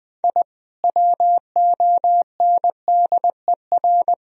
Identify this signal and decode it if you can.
Morse code, heard as a keyed tone, about 21 wpm, 700 Hz. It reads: I WONDER
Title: Sentences from Top 1000 Words - Review Part 3 - 20wpm